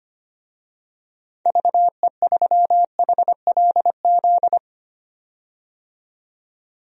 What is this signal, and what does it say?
Morse code, heard as a keyed tone, about 25 wpm, 700 Hz. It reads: VE3HLZ
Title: DXCC-100 Call Signs - 6 Characters - 25wpm